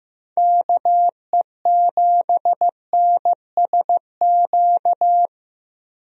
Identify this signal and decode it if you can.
Morse code, heard as a keyed tone, about 15 wpm, 700 Hz. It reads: KE7NSQ
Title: USA Call Signs - 6 Characters - Repeated 3x - 15wpm